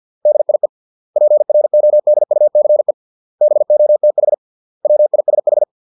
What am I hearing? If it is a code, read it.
Morse code, heard as a keyed tone, about 50 wpm, 600 Hz. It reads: DIE PRODUCE BOTH WISH